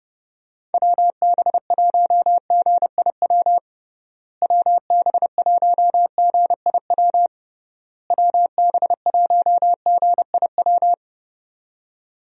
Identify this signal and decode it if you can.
Morse code, heard as a keyed tone, about 32 wpm, 700 Hz. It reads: WB1GIW WB1GIW WB1GIW
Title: USA Call Signs - 6 Characters - Repeated 3x - 30wpm